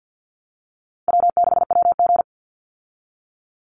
Morse code, heard as a keyed tone, about 50 wpm, 700 Hz. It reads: W6WG
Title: Call Signs Repeated 2x - 50wpm